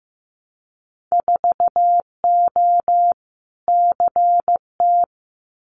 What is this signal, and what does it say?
Morse code, heard as a keyed tone, about 15 wpm, 700 Hz. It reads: 4O CT
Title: Winter Field Day Exchanges Encore 15wpm